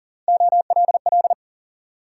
Morse code, heard as a keyed tone, about 56 wpm, spaced as for 48 wpm, 700 Hz. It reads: OLL
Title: Speed-Racing Multi-letter Phonemes - 40wpm